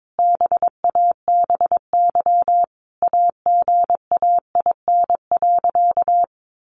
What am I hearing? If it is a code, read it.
Morse code, heard as a keyed tone, about 22 wpm, 700 Hz. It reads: BABY AGAIN.
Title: Sentences from Top 1300 Words 22wpm